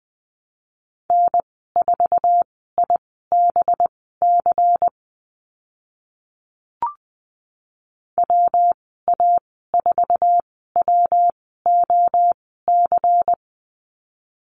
Morse code, heard as a keyed tone, about 20 wpm, spaced as for 14 wpm, 700 Hz. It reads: N4IBC WA4WOC
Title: 193 Lesson 31g - Letter v - Call Signs 20wpm